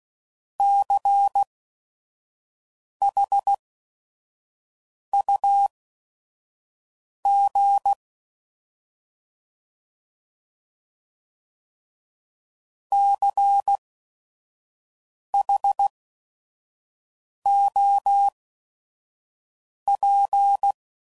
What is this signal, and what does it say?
Morse code, heard as a keyed tone, about 16 wpm, spaced as for 5 wpm, 800 Hz. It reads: CHUG CHOP
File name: Prefixes / By First Letter / Cs